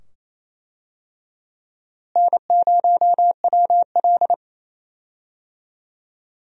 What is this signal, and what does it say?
Morse code, heard as a keyed tone, about 28 wpm, 700 Hz. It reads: N0WL
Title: USA Call Signs - 4 Characters - Repeated 2x - 28wpm